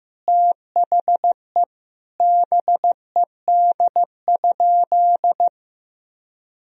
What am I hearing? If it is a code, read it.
Morse code, heard as a keyed tone, about 16 wpm, 700 Hz. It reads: THE BED?
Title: Sentences from Top 1000 Words - Review Part 4 - 15wpm